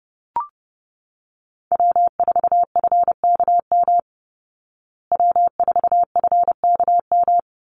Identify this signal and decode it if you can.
Morse code, heard as a keyed tone, about 30 wpm, 700 Hz. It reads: W4FKM W4FKM
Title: Call Signs Repeated 3x - 30wpm